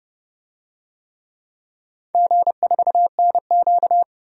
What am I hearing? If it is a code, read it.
Morse code, heard as a keyed tone, about 31 wpm, 700 Hz. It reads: G4NQ
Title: DXCC-100 Call Signs - 4 Characters - 30wpm